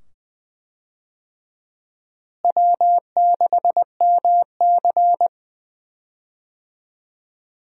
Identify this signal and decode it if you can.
Morse code, heard as a keyed tone, about 20 wpm, 700 Hz. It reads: W6MC
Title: USA Call Signs - 4 Characters - Repeated 2x - 20wpm